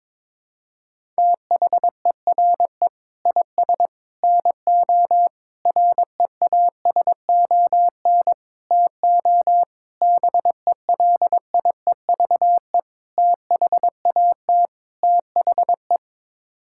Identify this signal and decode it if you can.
Morse code, heard as a keyed tone, about 22 wpm, 700 Hz. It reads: THERE IS NO REASON TO BELIEVE THAT THE
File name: practice / 22wpm / Sets of 8 Words - 22wpm